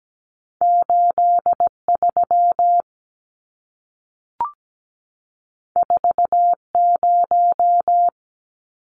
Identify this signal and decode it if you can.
Morse code, heard as a keyed tone, about 17 wpm, 700 Hz. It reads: 83 40
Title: Numbers - 2-Digit 17wpm